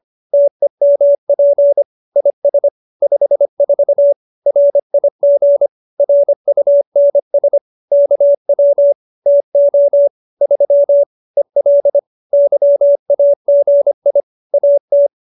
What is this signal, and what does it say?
Morse code, heard as a keyed tone, about 25 wpm, 570 Hz. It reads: TEMP IS 54 RIG RUNS KW TO 3 EL YAGI AT